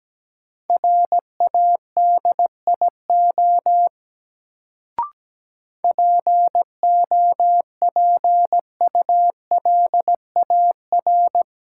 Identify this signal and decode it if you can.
Morse code, heard as a keyed tone, about 17 wpm, 700 Hz. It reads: RADIO POPULAR
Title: Sentences from Top 1000 Words - Review Part 3 - New Word Drill - 17wpm